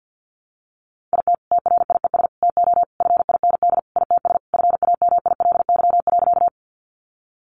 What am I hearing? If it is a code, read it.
Morse code, heard as a keyed tone, about 50 wpm, 700 Hz. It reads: IT TRIES TO FIND ITS FAMILY.
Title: Additional Easy Phrases 50wpm